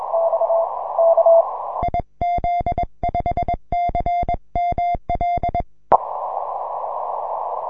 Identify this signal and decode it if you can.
Morse code, heard as a keyed tone, about 21 wpm, 2100 Hz, with noise. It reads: IZ5CML
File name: simple-file-list / UN3GX 6M